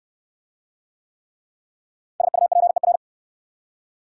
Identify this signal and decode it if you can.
Morse code, heard as a keyed tone, about 66 wpm, 690 Hz. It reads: SUPER